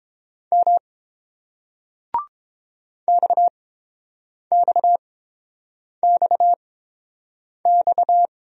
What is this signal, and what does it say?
Morse code, heard as a keyed tone, about 30 wpm, spaced as for 7 wpm, 700 Hz. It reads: M XXXX